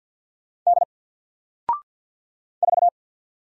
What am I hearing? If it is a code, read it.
Morse code, heard as a keyed tone, about 48 wpm, 700 Hz. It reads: D 4
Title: Single Letter-Number Alt 50wpm